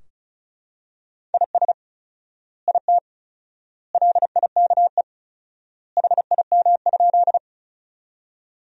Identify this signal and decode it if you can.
Morse code, heard as a keyed tone, about 34 wpm, 700 Hz. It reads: IS IT LIKE HIM?
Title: 4x Word-Spacing - Sentences from Top 100 Words - 35wpm